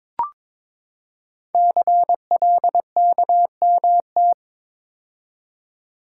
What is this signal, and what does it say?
Morse code, heard as a keyed tone, about 22 wpm, 700 Hz. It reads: CLKMT
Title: Code Groups - 5 Letters - 22wpm